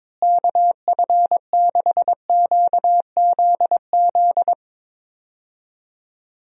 Morse code, heard as a keyed tone, about 22 wpm, 700 Hz. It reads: KF6QZZ
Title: USA Call Signs - 6 Characters - Repeated 2x - 22wpm